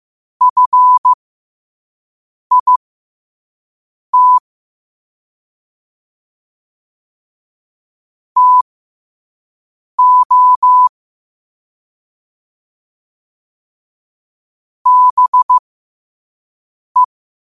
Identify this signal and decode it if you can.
Morse code, heard as a keyed tone, about 15 wpm, spaced as for 5 wpm, 1000 Hz. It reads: FIT TO BE
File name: qotdfiles / 5wpm